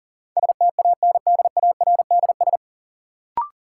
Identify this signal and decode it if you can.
Morse code, heard as a keyed tone, about 40 wpm, 700 Hz. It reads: STANDARDS
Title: Sentences from Top 1000 Words - Review Part 6 - New Word Drill - 40wpm